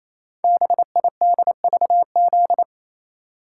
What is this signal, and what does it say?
Morse code, heard as a keyed tone, about 29 wpm, 700 Hz. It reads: BIDVZ
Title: Code Groups - 5 Letter:Numbers - 28wpm